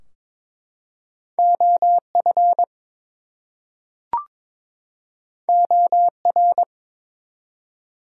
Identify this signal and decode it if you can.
Morse code, heard as a keyed tone, about 23 wpm, 700 Hz. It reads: OF OR
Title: Two Letter Words - 22wpm